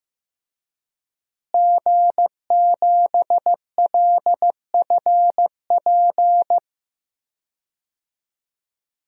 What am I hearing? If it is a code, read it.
Morse code, heard as a keyed tone, about 15 wpm, 700 Hz. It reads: G7LFP
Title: DXCC-100 Call Signs - 5 Characters - Repeated 3x - 15wpm